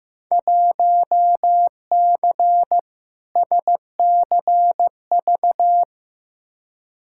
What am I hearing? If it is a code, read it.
Morse code, heard as a keyed tone, about 15 wpm, 700 Hz. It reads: 1C SCV
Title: Field Day Exchanges - 15wpm